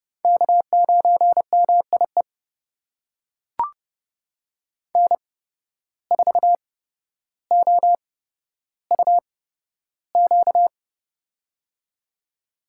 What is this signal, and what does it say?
Morse code, heard as a keyed tone, about 30 wpm, spaced as for 13 wpm, 700 Hz. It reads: K9MIE N4OUQ
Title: USA Call Signs - 5 Characters - 8x Character Spacing - 30wpm